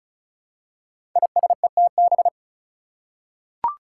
Morse code, heard as a keyed tone, about 36 wpm, 700 Hz. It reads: ISETB
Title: Five Challenging Letter-Number Combinations 35wpm